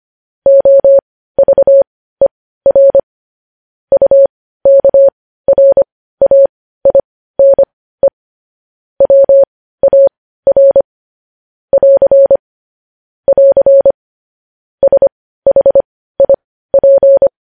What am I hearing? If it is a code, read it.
Morse code, heard as a keyed tone, about 25 wpm, spaced as for 15 wpm, 550 Hz. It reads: OVER UKRAINE WAR <AR> <AR> SHIP
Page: news headlines in morse code at 15 wpm